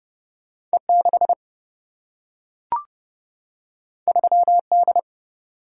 Morse code, heard as a keyed tone, about 29 wpm, 700 Hz. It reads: E 6 3 D
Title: Single-Letter Single-Number 30wpm